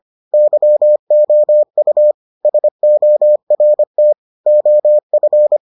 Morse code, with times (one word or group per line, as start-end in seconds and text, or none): 0.3-2.1 YOU
2.5-4.1 SORT
4.5-5.6 OF